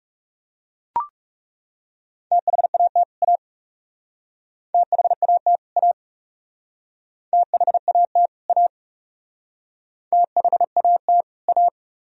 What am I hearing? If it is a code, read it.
Morse code, heard as a keyed tone, about 39 wpm, 700 Hz. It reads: THAT A THAT A THAT A THAT A